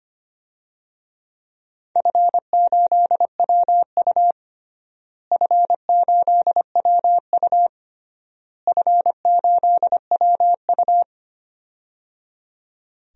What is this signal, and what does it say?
Morse code, heard as a keyed tone, about 25 wpm, 700 Hz. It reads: F8WU F8WU F8WU
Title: DXCC-100 Call Signs - 4 Characters - Repeated 3x - 25wpm